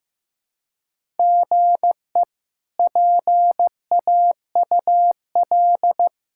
Morse code, heard as a keyed tone, about 15 wpm, 700 Hz. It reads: GE PAUL
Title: K1USN SST QSO Practice - 15wpm